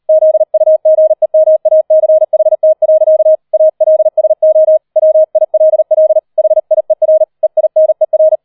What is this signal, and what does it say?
Morse code, heard as a keyed tone, about 38 wpm, 630 Hz. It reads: ZUGEMACHT. ALSO WILL HIER EINER